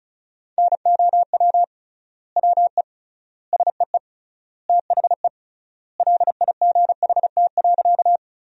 Morse code, heard as a keyed tone, about 36 wpm, 700 Hz. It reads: NOW WE SEE THE LIGHT.
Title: 3x Word-Spacing - Sentences from Top 200 Words - 35wpm